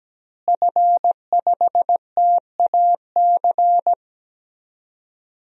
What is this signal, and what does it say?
Morse code, heard as a keyed tone, about 17 wpm, 700 Hz. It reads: F5TAC
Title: DXCC-100 Call Signs - 5 Characters - Repeated 2x - 17wpm